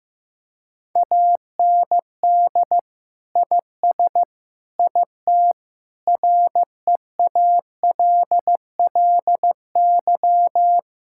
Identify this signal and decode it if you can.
Morse code, heard as a keyed tone, about 15 wpm, 700 Hz. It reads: AND IS IT REALLY
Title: Sentences from Top 1600 Words 15wpm